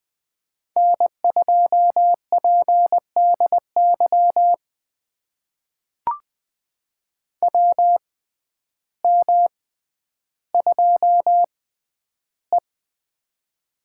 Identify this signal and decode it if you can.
Morse code, heard as a keyed tone, about 20 wpm, spaced as for 11 wpm, 700 Hz. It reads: N2PDY WM2E
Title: USA Call Signs - 6x Character Spacing - 20wpm